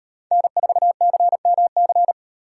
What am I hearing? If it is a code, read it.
Morse code, heard as a keyed tone, about 36 wpm, 700 Hz. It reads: N4CMC